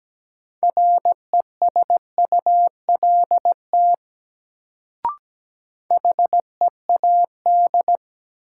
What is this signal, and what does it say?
Morse code, heard as a keyed tone, about 17 wpm, 700 Hz. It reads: RESULT HEAD
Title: Sets of 3 Words - New Word Drill - 17wpm